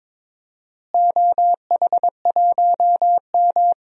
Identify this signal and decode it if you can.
Morse code, heard as a keyed tone, about 22 wpm, 700 Hz. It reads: OH1M
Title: DXCC-100 Call Signs - 4 Characters - Repeated 2x - 22wpm